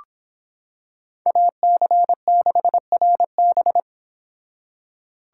Morse code, heard as a keyed tone, about 25 wpm, 700 Hz. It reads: AC6RB